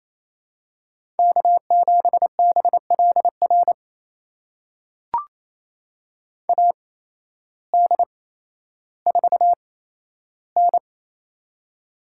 Morse code, heard as a keyed tone, about 28 wpm, spaced as for 13 wpm, 700 Hz. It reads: K7BLR AD4N
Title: USA Call Signs - 8x Character Spacing - 28wpm